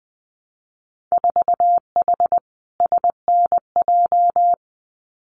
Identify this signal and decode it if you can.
Morse code, heard as a keyed tone, about 20 wpm, 700 Hz. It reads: 4H SNJ
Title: Winter Field Day Exchanges Encore 20wpm